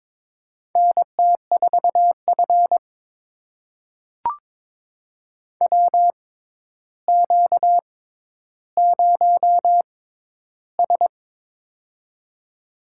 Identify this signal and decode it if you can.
Morse code, heard as a keyed tone, about 23 wpm, spaced as for 11 wpm, 700 Hz. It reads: NT4F WQ0S
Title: USA Call Signs - 4 Characters - 6x Character Spacing - 22wpm